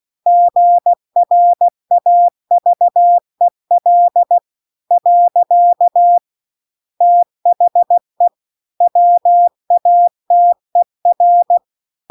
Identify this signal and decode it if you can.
Morse code, heard as a keyed tone, about 16 wpm, 700 Hz. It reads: GRAVEL . THE WATER
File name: CW-The-Big-Twoheaded-River-2-16WPM